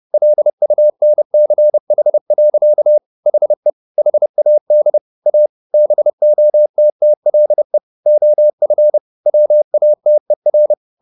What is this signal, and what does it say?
Morse code, heard as a keyed tone, about 30 wpm, 600 Hz. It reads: LUNCH. HE HAD A BOTTLE OF WATER
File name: the old man and the sea / the old man and the sea 1 30wpm